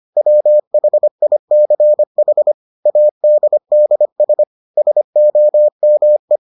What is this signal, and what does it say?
Morse code, heard as a keyed tone, about 25 wpm, 600 Hz. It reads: WHICH ADDS SOME